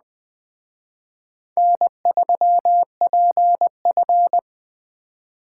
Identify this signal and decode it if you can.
Morse code, heard as a keyed tone, about 20 wpm, 700 Hz. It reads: N3PF
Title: Speed Racing - USA 1x2 Call Signs 17wpm